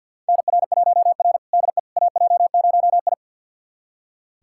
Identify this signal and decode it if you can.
Morse code, heard as a keyed tone, about 46 wpm, 690 Hz. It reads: NR1R DE AJ0I
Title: USA Callsign DE Callsign - 4 Characters - 50wpm